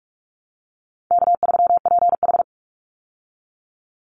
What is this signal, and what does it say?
Morse code, heard as a keyed tone, about 44 wpm, 700 Hz. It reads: K3PH
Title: Easy Call Signs Repeated 3x - 45wpm